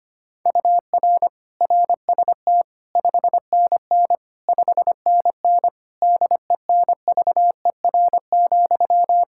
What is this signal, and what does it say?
Morse code, heard as a keyed tone, about 25 wpm, 700 Hz. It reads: UR RST 5NN 5NN DENVER,
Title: Practice QSO - Exchange 1 - Beginning - 25wpm